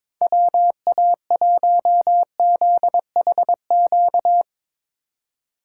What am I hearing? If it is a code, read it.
Morse code, heard as a keyed tone, about 21 wpm, 700 Hz. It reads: WA1ZHQ
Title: USA Call Signs - 6 Characters - 2x Character Spacing - 22wpm